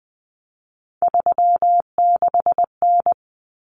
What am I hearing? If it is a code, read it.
Morse code, heard as a keyed tone, about 20 wpm, 700 Hz. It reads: 36N
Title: Common Cut Numbers - 3-Digit 20wpm